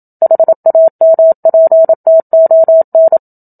Morse code, heard as a keyed tone, about 27 wpm, 650 Hz. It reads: HAMPTON
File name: Book Two / 01-Chapter 1 - 27 WPM